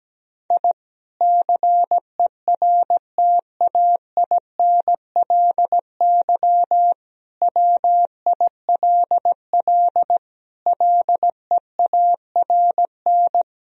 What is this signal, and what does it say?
Morse code, heard as a keyed tone, about 17 wpm, 700 Hz. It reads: I CERTAINLY WILL LEARN
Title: Sentences from Top 1000 Words - Review Part 4 - 17wpm